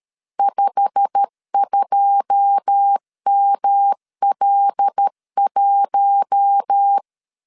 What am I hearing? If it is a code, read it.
Morse code, heard as a keyed tone, about 13 wpm, 800 Hz. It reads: 52ML1
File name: LEZIONI / 9 2 12wpm 2301